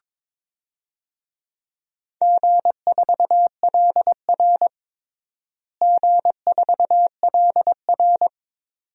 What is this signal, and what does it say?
Morse code, heard as a keyed tone, about 22 wpm, 700 Hz. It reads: G4LR G4LR
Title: DXCC-100 Call Signs - 4 Characters - Repeated 2x - 22wpm